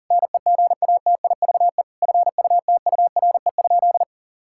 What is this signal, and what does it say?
Morse code, heard as a keyed tone, about 37 wpm, 700 Hz. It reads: NEGATIVE FUTURE?
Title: Sentences from Top 1000 Words - Review Part 6 - 40wpm